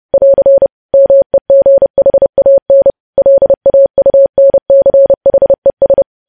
Morse code, heard as a keyed tone, about 30 wpm, 550 Hz. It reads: <AR> MEGHAN LAUNCHES
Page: news headlines in morse code at 30 wpm